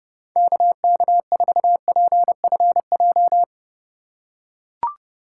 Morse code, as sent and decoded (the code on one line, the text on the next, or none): -.- -.- ....- .--. ..-. .---
KK4PFJ